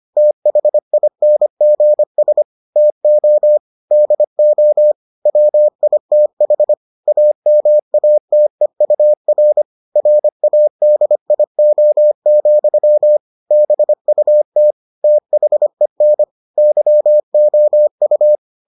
Morse code, heard as a keyed tone, about 25 wpm, 600 Hz. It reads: THINGS TO DO WITH AMATEUR RADIO, BUT THEN YOU